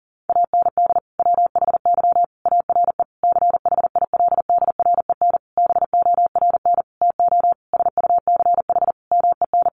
Morse code, heard as a keyed tone, about 40 wpm, 700 Hz. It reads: AND WHY ARE CHILDREN BORN TO SUCH MEN
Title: Sentences from Top 1000 Words 40wpm